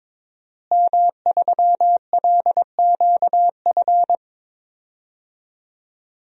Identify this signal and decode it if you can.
Morse code, heard as a keyed tone, about 22 wpm, 700 Hz. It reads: M3LQF
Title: DXCC-100 Call Signs - 5 Characters - Repeated 2x - 22wpm